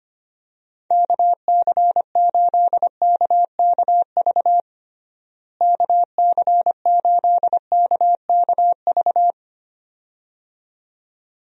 Morse code, heard as a keyed tone, about 25 wpm, 700 Hz. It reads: KC8KKV KC8KKV
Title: USA Call Signs - 6 Characters - Repeated 2x - 25wpm